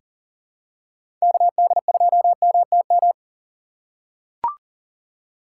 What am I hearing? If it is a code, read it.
Morse code, heard as a keyed tone, about 38 wpm, 700 Hz. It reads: KD2MTM
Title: USA Call Signs - 6 Characters - 4x Character Spacing - 40wpm